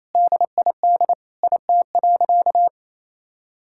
Morse code, heard as a keyed tone, about 28 wpm, 700 Hz. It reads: DID IT.
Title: Sentences from Top 1000 Words - Review Part 6 - 28wpm